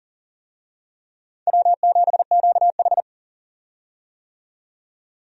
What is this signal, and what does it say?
Morse code, heard as a keyed tone, about 37 wpm, 700 Hz. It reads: W7QH
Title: USA Call Signs - 4 Characters - 40wpm